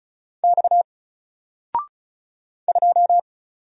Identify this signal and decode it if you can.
Morse code, heard as a keyed tone, about 36 wpm, 700 Hz. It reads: X 2
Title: Single Character (Letter:Number:Punctuation:Prosign) 35wpm